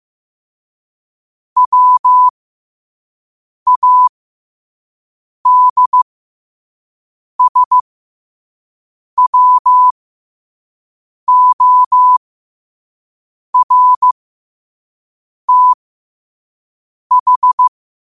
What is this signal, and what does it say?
Morse code, heard as a keyed tone, about 15 wpm, spaced as for 5 wpm, 1000 Hz. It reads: WADSWORTH
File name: qotdfiles / 5wpm